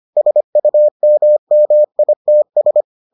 Morse code, heard as a keyed tone, about 25 wpm, 600 Hz. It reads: SUMMITS